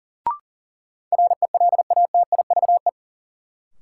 Morse code, heard as a keyed tone, about 40 wpm, 700 Hz. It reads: RELATIVE
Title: Sentences from Top 1400 Words - New Word Drill - 40wpm